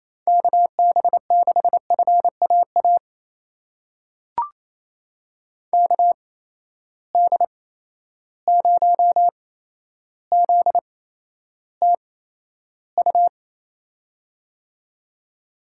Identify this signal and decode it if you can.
Morse code, heard as a keyed tone, about 28 wpm, spaced as for 12 wpm, 700 Hz. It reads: KB6FAA KD0ZTU
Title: USA Call Signs - 6 Characters - 8x Character Spacing - 28wpm